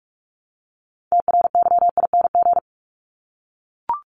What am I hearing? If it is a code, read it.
Morse code, heard as a keyed tone, about 44 wpm, 700 Hz. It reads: TRYING
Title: ING Suffix - Top 100 Words with Base Words 45wpm